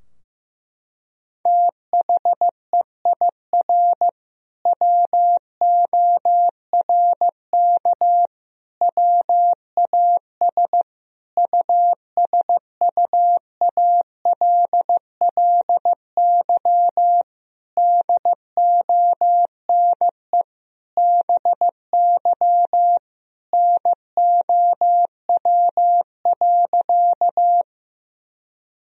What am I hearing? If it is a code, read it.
Morse code, heard as a keyed tone, about 15 wpm, 700 Hz. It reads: THEIR WORK WAS USUALLY DONE BY NOW.
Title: Sentences from Top 1000 Words - Review Part 3 - 15wpm